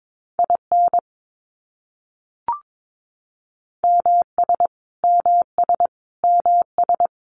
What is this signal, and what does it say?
Morse code, heard as a keyed tone, about 22 wpm, 700 Hz. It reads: IN MS MS MS
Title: US States Repeated 3x 22wpm